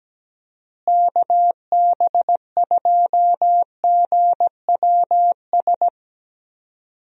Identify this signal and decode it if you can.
Morse code, heard as a keyed tone, about 17 wpm, 700 Hz. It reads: KB2GWS
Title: USA Call Signs - 6 Characters - Repeated 2x - 17wpm